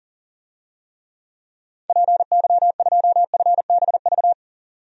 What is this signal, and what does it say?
Morse code, heard as a keyed tone, about 40 wpm, 700 Hz. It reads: PY2FBV
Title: DXCC-100 Call Signs - 6 Characters - 40wpm